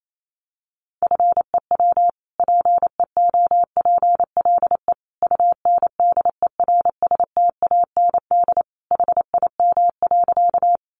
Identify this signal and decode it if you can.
Morse code, heard as a keyed tone, about 28 wpm, 700 Hz. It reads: FEW PEOPLE UNDERSTAND HIM.
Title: Sentences from Top 600 Words 28wpm